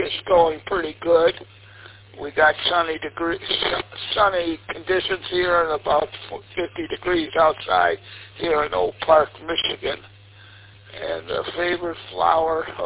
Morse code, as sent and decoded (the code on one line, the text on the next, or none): none